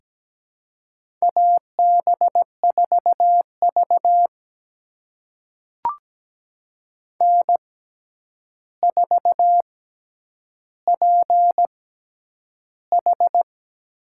Morse code, heard as a keyed tone, about 17 wpm, spaced as for 9 wpm, 700 Hz. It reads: AB4V N4PH
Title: USA Call Signs - 4 Characters - 6x Character Spacing - 17wpm